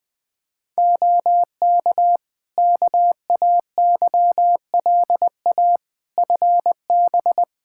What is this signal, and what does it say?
Morse code, heard as a keyed tone, about 20 wpm, 700 Hz. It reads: OK KAYLA FB
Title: K1USN SST QSO Practice 20wpm